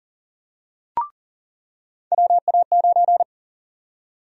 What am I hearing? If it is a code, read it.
Morse code, heard as a keyed tone, about 40 wpm, 700 Hz. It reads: WA9
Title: USA Call Sign Prefixes - 40wpm